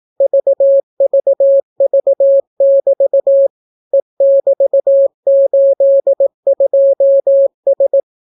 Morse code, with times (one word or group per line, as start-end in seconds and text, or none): 0.2-3.5 VVV<BT>
3.9-8.0 E<BT>82S